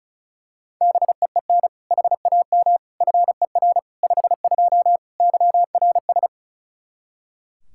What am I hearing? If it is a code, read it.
Morse code, heard as a keyed tone, about 34 wpm, 700 Hz. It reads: BEEN HAM FER 52 YRS
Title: Practice QSO - Exchange 3 - More Info - 35wpm